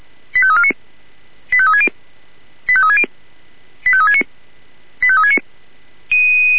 Five-tone selective call (ZVEI), heard as quick runs of five tones, tones of about 70 ms. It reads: 85349 85360 85360 85369 85369